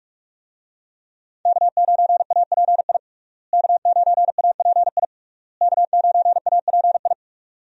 Morse code, heard as a keyed tone, about 46 wpm, 700 Hz. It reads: K9API K9API K9API
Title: USA Call Signs - 5 Characters - Repeated 3x - 45wpm